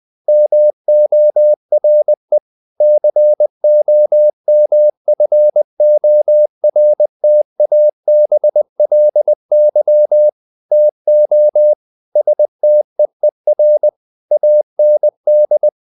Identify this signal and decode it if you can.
Morse code, heard as a keyed tone, about 20 wpm, 600 Hz. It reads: MORE COMFORTABLY TO STEER AND
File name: the old man and the sea part 2 20wpm